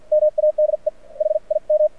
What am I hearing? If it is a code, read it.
Morse code, heard as a keyed tone, about 53 wpm, spaced as for 43 wpm, 610 Hz. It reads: MADE HIM
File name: archived / CRSnet